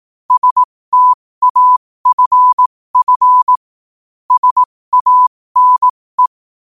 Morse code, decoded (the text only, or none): STAFF SANE